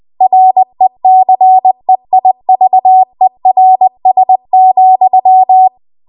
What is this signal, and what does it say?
Morse code, heard as a keyed tone, about 20 wpm, 750 Hz. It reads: RECEIVERS,
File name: Archive / 20 WPM / 20WPM